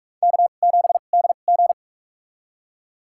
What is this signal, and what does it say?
Morse code, heard as a keyed tone, about 47 wpm, spaced as for 33 wpm, 700 Hz. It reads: K7DG
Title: USA Call Signs - 4 Characters - 2x Character Spacing - 45wpm